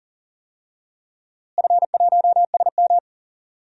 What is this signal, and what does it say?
Morse code, heard as a keyed tone, about 40 wpm, 700 Hz. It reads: F1SM
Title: DXCC-100 Call Signs - 4 Characters - Repeated 2x - 40wpm